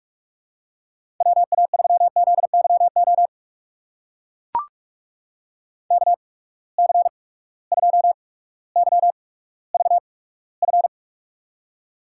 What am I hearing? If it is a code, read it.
Morse code, heard as a keyed tone, about 43 wpm, spaced as for 19 wpm, 700 Hz. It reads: WA3ZYO KC2YVF
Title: USA Call Signs - 6 Characters - 8x Character Spacing - 45wpm